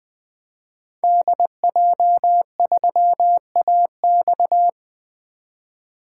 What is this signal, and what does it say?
Morse code, heard as a keyed tone, about 20 wpm, 700 Hz. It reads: DJ3AX
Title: 2023 SOTA Activator Call Signs - 20wpm